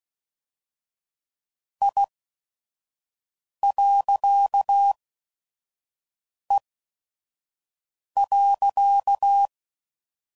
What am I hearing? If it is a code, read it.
Morse code, heard as a keyed tone, about 16 wpm, spaced as for 5 wpm, 800 Hz. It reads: I.E.
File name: CW Code Groups @5WPM / Technical Subjects / Reactance